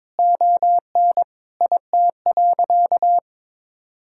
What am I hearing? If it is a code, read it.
Morse code, heard as a keyed tone, about 22 wpm, 700 Hz. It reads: ON IT.
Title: Sentences from Top 1600 Words 22wpm